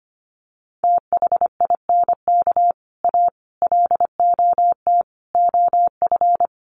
Morse code, heard as a keyed tone, about 25 wpm, 700 Hz. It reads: THINK A LOT OF
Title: Sets of 5 Words 25wpm